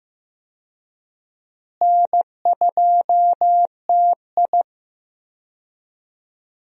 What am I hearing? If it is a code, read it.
Morse code, heard as a keyed tone, about 15 wpm, 700 Hz. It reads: N2TI
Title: USA Call Signs - 4 Characters - Repeated 2x - 15wpm